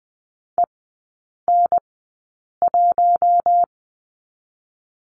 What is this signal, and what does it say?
Morse code, heard as a keyed tone, about 20 wpm, spaced as for 8 wpm, 700 Hz. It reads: EN1
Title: 047 Lesson 9e - Number 4 - Three Characters 20wpm